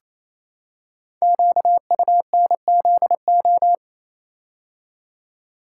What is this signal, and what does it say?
Morse code, heard as a keyed tone, about 27 wpm, 700 Hz. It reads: QUNZO
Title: Code Groups - 5 Letters - 28wpm